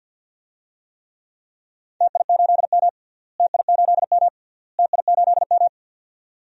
Morse code, heard as a keyed tone, about 50 wpm, 700 Hz. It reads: TI8M TI8M TI8M
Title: DXCC-100 Call Signs - 4 Characters - Repeated 3x - 50wpm